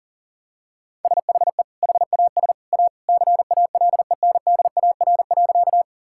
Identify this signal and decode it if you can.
Morse code, heard as a keyed tone, about 37 wpm, 700 Hz. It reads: SHE HAS A CALENDAR.
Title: Sentences from Top 1600 Words 40wpm